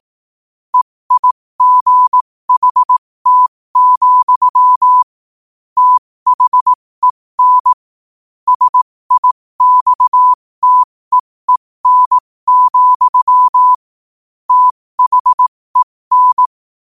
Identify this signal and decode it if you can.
Morse code, heard as a keyed tone, about 18 wpm, spaced as for 15 wpm, 1000 Hz. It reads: EIGHT, THEN SIXTEEN, THEN
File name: qotdfiles / 15wpm